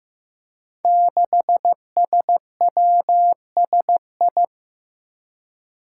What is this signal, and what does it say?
Morse code, heard as a keyed tone, about 15 wpm, 700 Hz. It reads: 6SWSI
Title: Five Challenging Letter-Number Combinations 15wpm